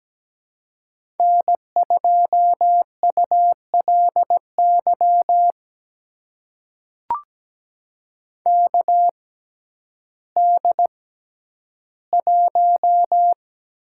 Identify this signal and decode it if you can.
Morse code, heard as a keyed tone, about 17 wpm, spaced as for 10 wpm, 700 Hz. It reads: N2ULY KD1
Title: USA Call Signs - 5 Characters - 6x Character Spacing - 17wpm